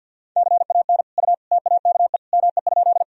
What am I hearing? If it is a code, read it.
Morse code, heard as a keyed tone, about 45 wpm, 700 Hz. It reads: CAN U TAKE ME?
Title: Sentences from Top 200 Words and Abbreviations 50wpm